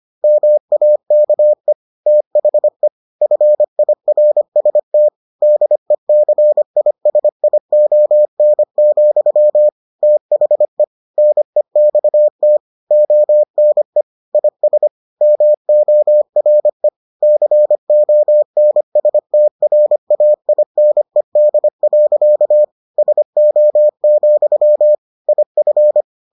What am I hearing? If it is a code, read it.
Morse code, heard as a keyed tone, about 25 wpm, 600 Hz. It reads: MAKE THE FIRST DECISION, THE NEXT ONE IS MORE CONSTRAINED. SO, IF